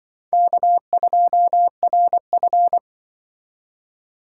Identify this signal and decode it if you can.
Morse code, heard as a keyed tone, about 24 wpm, 700 Hz. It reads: K2RF
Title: Speed Racing - USA 1x2 Call Signs 20wpm